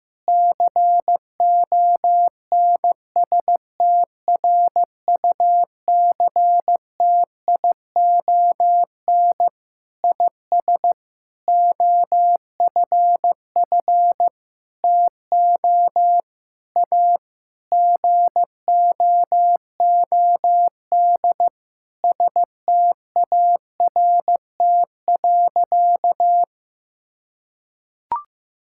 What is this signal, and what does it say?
Morse code, heard as a keyed tone, about 15 wpm, 700 Hz. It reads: CONSTRUCTION IS OFF TO A GOOD START.
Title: Sentences from Top 1600 Words 15wpm